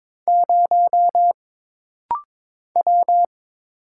Morse code, heard as a keyed tone, about 22 wpm, 700 Hz. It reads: 0 W
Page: This rapid-fire timing is geared to push your ICR (Instant Character Recognition) skills.